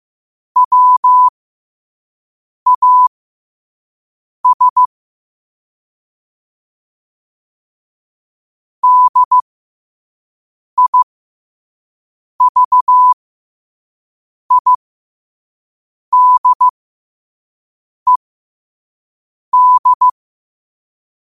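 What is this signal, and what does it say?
Morse code, heard as a keyed tone, about 15 wpm, spaced as for 5 wpm, 1000 Hz. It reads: WAS DIVIDED
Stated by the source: Quote-of-the-day Morse code practice at 5 wpm.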